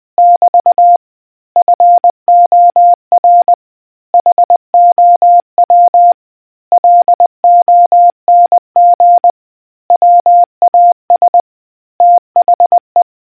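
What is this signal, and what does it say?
Morse code, heard as a keyed tone, about 20 wpm, 700 Hz. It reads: <BT> FOR HOW LONG WAS THE
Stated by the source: Morse quiz question from a New Zealand navy communicators association.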